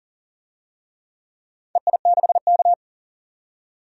Morse code, heard as a keyed tone, about 40 wpm, 700 Hz. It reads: EI6K
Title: DXCC-100 Call Signs - 4 Characters - 40wpm